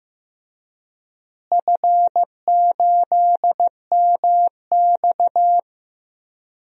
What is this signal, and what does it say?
Morse code, heard as a keyed tone, about 15 wpm, 700 Hz. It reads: F8MX
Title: DXCC-100 Call Signs - 4 Characters - Repeated 3x - 15wpm